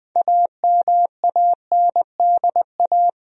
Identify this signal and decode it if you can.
Morse code, heard as a keyed tone, about 20 wpm, 700 Hz. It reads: AMANDA
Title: USA Common Names - 20wpm